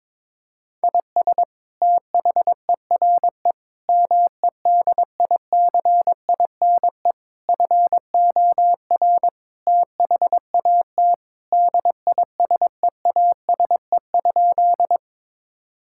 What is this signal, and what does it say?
Morse code, heard as a keyed tone, about 22 wpm, 700 Hz. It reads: IS THERE MEDICINE FOR THAT DISEASE?
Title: Sentences from Top 1000 Words - Review Part 6 - 22wpm